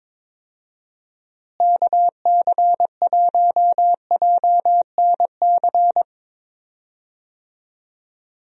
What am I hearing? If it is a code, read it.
Morse code, heard as a keyed tone, about 22 wpm, 700 Hz. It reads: KC1JNC
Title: USA Call Signs - 6 Characters - Repeated 2x - 22wpm